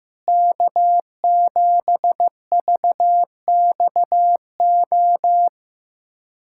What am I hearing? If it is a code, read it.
Morse code, heard as a keyed tone, about 15 wpm, 700 Hz. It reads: K7VXO
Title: USA Callsign DE Callsign - 5 Characters - 15wpm